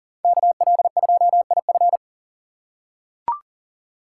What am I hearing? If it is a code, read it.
Morse code, heard as a keyed tone, about 40 wpm, 700 Hz. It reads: KL2IF
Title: USA Call Signs - 5 Characters - 3x Character Spacing - 40wpm